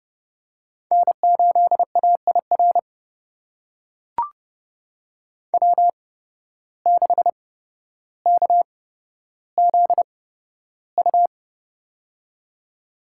Morse code, heard as a keyed tone, about 31 wpm, spaced as for 13 wpm, 700 Hz. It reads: N8AIR W6KZU
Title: USA Call Signs - 5 Characters - 8x Character Spacing - 30wpm